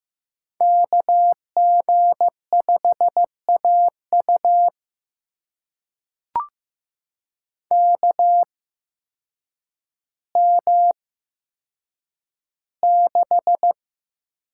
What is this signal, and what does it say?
Morse code, heard as a keyed tone, about 15 wpm, spaced as for 8 wpm, 700 Hz. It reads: KG5AU KM6
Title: USA Call Signs - 8x Character Spacing - 15wpm